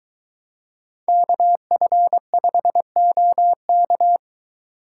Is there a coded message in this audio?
Yes: Morse code from 1.1 to 4.2 s, heard as a keyed tone, about 22 wpm, 700 Hz. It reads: KF5OK